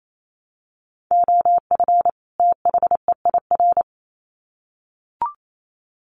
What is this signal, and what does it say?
Morse code, heard as a keyed tone, about 28 wpm, 700 Hz. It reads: OF THEIR
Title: Sets of 2 Words 28wpm